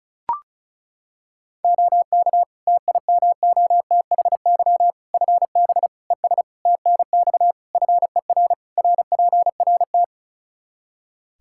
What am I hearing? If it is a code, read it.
Morse code, heard as a keyed tone, about 34 wpm, 700 Hz. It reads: OK TIMOTHY FB ES TNX FER RPRT
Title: K1USN SST QSO Practice 35wpm